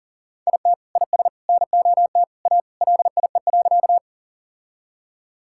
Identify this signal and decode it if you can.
Morse code, heard as a keyed tone, about 40 wpm, 700 Hz. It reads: IT IS NOT A LIE.
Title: Sentences from Top 1600 Words 40wpm